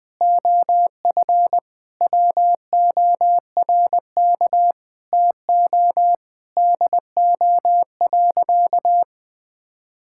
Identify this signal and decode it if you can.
Morse code, heard as a keyed tone, about 20 wpm, 700 Hz. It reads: OF WORK TO DO.
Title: Sentences with 6-Word Phrases - 20wpm